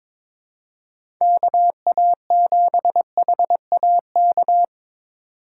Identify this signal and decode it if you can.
Morse code, heard as a keyed tone, about 22 wpm, 700 Hz. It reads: KA7HAK
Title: USA Call Signs - 6 Characters - 6x Character Spacing - 22wpm